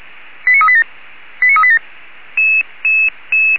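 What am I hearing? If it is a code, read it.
Five-tone selective call (ZVEI), heard as quick runs of five tones, tones of about 70 ms. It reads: 89387 89387
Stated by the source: ZVEI five-tone call received on SDR.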